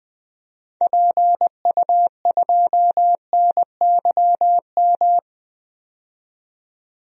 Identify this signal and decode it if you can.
Morse code, heard as a keyed tone, about 20 wpm, 700 Hz. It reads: PU2NYM
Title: DXCC-100 Call Signs - 6 Characters - 20wpm